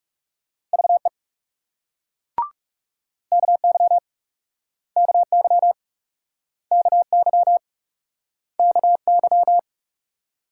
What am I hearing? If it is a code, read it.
Morse code, heard as a keyed tone, about 42 wpm, 700 Hz. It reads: VE KY KY KY KY